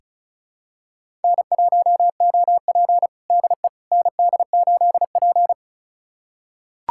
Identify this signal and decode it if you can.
Morse code, heard as a keyed tone, about 34 wpm, 700 Hz. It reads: N1OP DE ND8P
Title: USA Callsign DE Callsign - 4 Characters - 35wpm